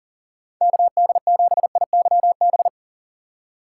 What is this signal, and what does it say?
Morse code, heard as a keyed tone, about 40 wpm, 700 Hz. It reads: KD7IYB
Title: USA Call Signs - 6 Characters - 3x Character Spacing - 40wpm